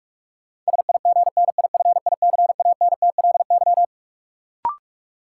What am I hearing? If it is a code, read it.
Morse code, heard as a keyed tone, about 42 wpm, 700 Hz. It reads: SIGNIFICANTLY
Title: Sentences from Top 1500 Words - New Word Drill Alt - 45wpm